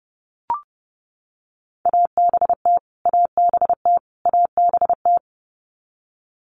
Morse code, heard as a keyed tone, about 30 wpm, 700 Hz. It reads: ABT ABT ABT
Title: Abbreviations Repeated 3x - 30wpm